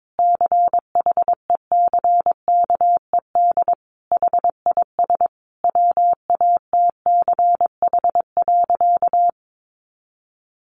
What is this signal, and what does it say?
Morse code, heard as a keyed tone, about 22 wpm, 700 Hz. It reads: CHECKED HIS WATCH.
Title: A Sound Of Thunder 22wpm